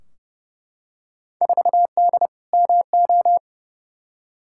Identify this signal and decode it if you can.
Morse code, heard as a keyed tone, about 29 wpm, 700 Hz. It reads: 4D MO
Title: Field Day Exchanges Encore - 30wpm